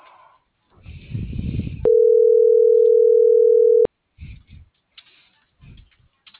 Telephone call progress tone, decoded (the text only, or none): ringback tone